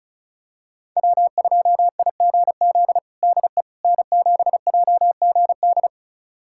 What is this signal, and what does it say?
Morse code, heard as a keyed tone, about 34 wpm, 700 Hz. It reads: W2IGZ DE N7JGD
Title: USA Callsign DE Callsign - 5 Characters - 35wpm